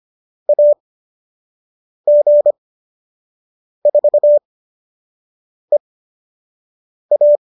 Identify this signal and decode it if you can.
Morse code, heard as a keyed tone, about 25 wpm, spaced as for 6 wpm, 590 Hz. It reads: AG4EA